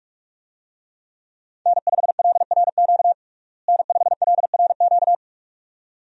Morse code, heard as a keyed tone, about 46 wpm, 700 Hz. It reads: N5LRQ N5LRQ
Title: USA Call Signs - 5 Characters - Repeated 2x - 45wpm